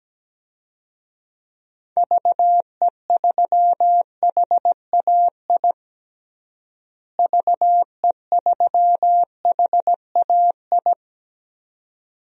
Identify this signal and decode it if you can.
Morse code, heard as a keyed tone, about 17 wpm, 700 Hz. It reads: VE3HAI VE3HAI
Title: DXCC-100 Call Signs - 6 Characters - Repeated 2x - 17wpm